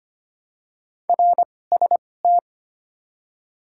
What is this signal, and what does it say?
Morse code, heard as a keyed tone, about 25 wpm, spaced as for 18 wpm, 700 Hz. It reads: RST
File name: Top 90 QSO Elements - 2x Character Spacing - 25wpm